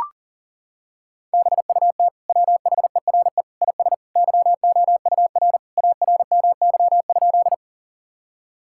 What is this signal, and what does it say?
Morse code, heard as a keyed tone, about 40 wpm, 700 Hz. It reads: BUT WHERE IS YOUR ARMY?
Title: Sentences from Top 1000 Words - Review Part 5 - 40wpm